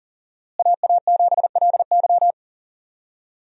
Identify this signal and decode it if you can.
Morse code, heard as a keyed tone, about 40 wpm, 700 Hz. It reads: AA7LY
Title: USA Call Signs - 5 Characters - Repeated 3x - 40wpm